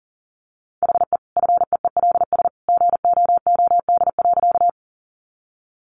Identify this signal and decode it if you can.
Morse code, heard as a keyed tone, about 40 wpm, 700 Hz. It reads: HE FEELS GOOD.
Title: Additional Easy Phrases 40wpm